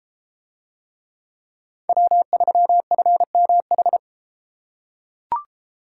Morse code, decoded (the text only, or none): W3FMH